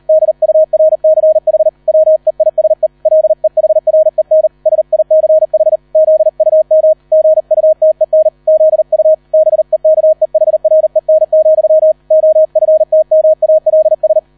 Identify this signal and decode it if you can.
Morse code, heard as a keyed tone, about 38 wpm, 630 Hz. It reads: DURCH WEISE LEHREN SICH ZUM GUTEN ZU BEKEHREN, OFTMALS